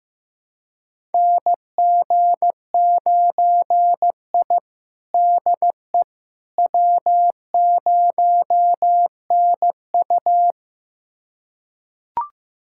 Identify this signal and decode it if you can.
Morse code, heard as a keyed tone, about 16 wpm, 700 Hz. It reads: NG9I DE W0NU
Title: USA Callsign DE Callsign - 4 Characters - 15wpm